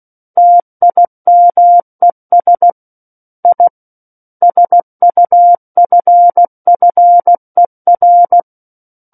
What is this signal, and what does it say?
Morse code, heard as a keyed tone, about 16 wpm, 700 Hz. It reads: TIMES I SUFFER
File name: Book One / 07-Chapter 7 - 16 WPM